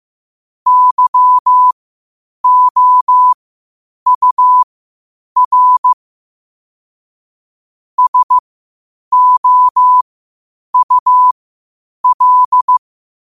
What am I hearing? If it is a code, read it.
Morse code, heard as a keyed tone, about 15 wpm, spaced as for 8 wpm, 1000 Hz. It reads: YOUR SOUL
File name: qotdfiles / 8wpm